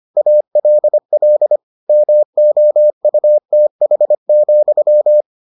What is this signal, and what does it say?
Morse code, heard as a keyed tone, about 25 wpm, 600 Hz. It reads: ALL MOUTH,